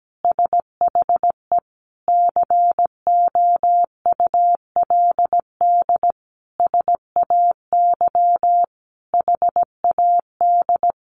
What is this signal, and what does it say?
Morse code, heard as a keyed tone, about 17 wpm, 700 Hz. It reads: SHE COULD SAY HAD
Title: Sentences from Top 1100 Words 17wpm